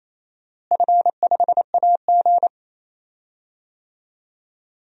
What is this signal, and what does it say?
Morse code, heard as a keyed tone, about 28 wpm, 700 Hz. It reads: F5AG
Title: DXCC-100 Call Signs - 4 Characters - Repeated 3x - 28wpm